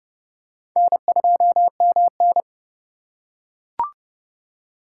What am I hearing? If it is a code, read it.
Morse code, heard as a keyed tone, about 28 wpm, 700 Hz. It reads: N2MN